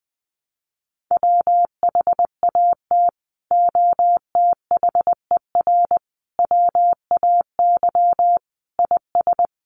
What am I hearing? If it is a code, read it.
Morse code, heard as a keyed tone, about 20 wpm, 700 Hz. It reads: WHAT OTHER WAY IS
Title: Sentences from Top 100 Words 20wpm